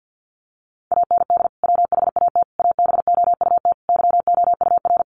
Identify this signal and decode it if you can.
Morse code, heard as a keyed tone, about 50 wpm, 700 Hz. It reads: AND WHAT ABOUT YOUR
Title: Sentences from Top 200 Words 50wpm